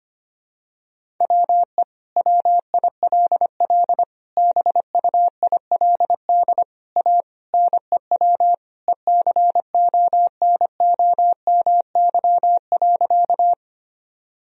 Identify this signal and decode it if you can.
Morse code, heard as a keyed tone, about 25 wpm, 700 Hz. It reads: WE WILL BUILD A NEW ECONOMY.
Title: Sentences from Top 1000 Words - Review Part 3 - 25wpm